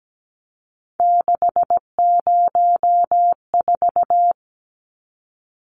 Morse code, heard as a keyed tone, about 17 wpm, 700 Hz. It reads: 604
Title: Numbers - 3-Digit 17wpm